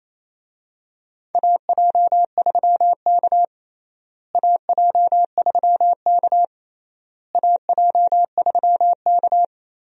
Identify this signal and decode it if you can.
Morse code, heard as a keyed tone, about 28 wpm, 700 Hz. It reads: AJ3K AJ3K AJ3K
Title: USA Call Signs - 4 Characters - Repeated 3x - 28wpm